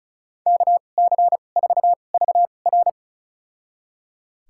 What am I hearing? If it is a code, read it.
Morse code, heard as a keyed tone, about 34 wpm, spaced as for 25 wpm, 700 Hz. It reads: KC4VR
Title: USA Call Signs - 5 Characters - 2x Character Spacing - 35wpm